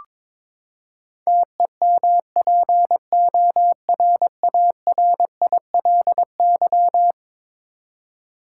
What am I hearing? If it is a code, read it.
Morse code, heard as a keyed tone, about 22 wpm, 700 Hz. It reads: TEMPORARILY
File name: practice / 22wpm / Eleven Letter Words - 22wpm